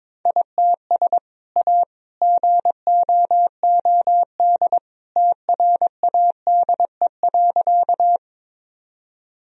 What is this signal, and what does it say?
Morse code, heard as a keyed tone, about 22 wpm, 700 Hz. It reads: ITS A GOOD TRADE.
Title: Sentences from Top 1000 Words - Review Part 4 - 22wpm